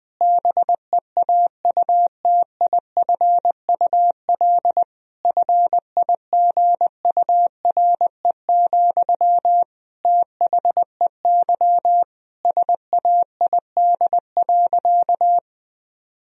Morse code, heard as a keyed tone, about 20 wpm, 700 Hz. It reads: BEAUTIFUL FIGURE, THEY SAID.
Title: Sentences from Top 1000 Words - Review Part 2 - 20wpm